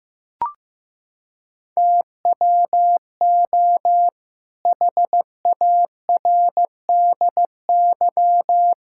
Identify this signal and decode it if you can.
Morse code, heard as a keyed tone, about 15 wpm, 700 Hz. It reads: TWO HARDY
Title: Pangram Challenge v2 - 15wpm